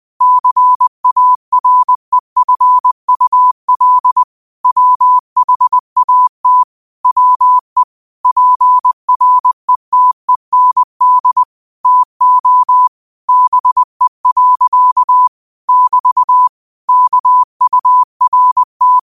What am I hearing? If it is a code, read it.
Morse code, heard as a keyed tone, about 20 wpm, 1000 Hz. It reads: CAREFUL WHAT WE PRETEND TO BE. <BT> KURT